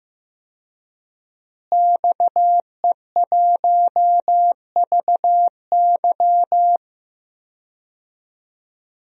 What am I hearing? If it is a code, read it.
Morse code, heard as a keyed tone, about 15 wpm, 700 Hz. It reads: XE1VY
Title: DXCC-100 Call Signs - 5 Characters - Repeated 2x - 15wpm